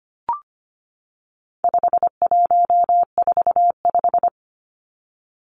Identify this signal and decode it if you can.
Morse code, heard as a keyed tone, about 25 wpm, 700 Hz. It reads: 5145
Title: Numbers - 4-Digit 25wpm